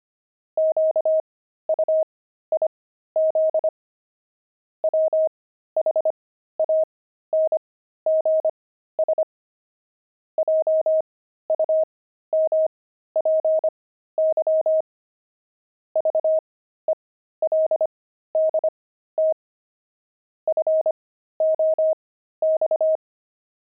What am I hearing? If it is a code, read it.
Morse code, heard as a keyed tone, about 25 wpm, spaced as for 13 wpm, 630 Hz. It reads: QUIZ WHANGS JUMPY VELDT FOX
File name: pangram-fs-13-cs-25-veldt